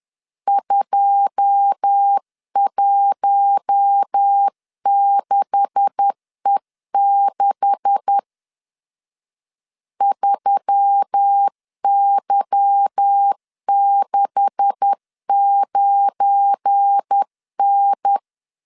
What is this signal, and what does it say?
Morse code, heard as a keyed tone, about 10 wpm, 800 Hz. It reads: 216E6 3Y69N
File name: LEZIONI / 9 286 10wpm 2735